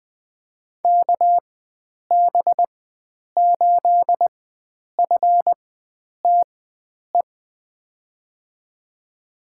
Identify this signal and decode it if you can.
Morse code, heard as a keyed tone, about 20 wpm, spaced as for 9 wpm, 700 Hz. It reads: KB8FTE
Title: USA Call Signs - 6 Characters - 4x Character Spacing - 20wpm